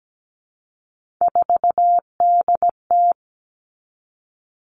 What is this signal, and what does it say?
Morse code, heard as a keyed tone, about 17 wpm, 700 Hz. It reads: 4DT
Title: Three Challenging Letter-Number Combinations 17wpm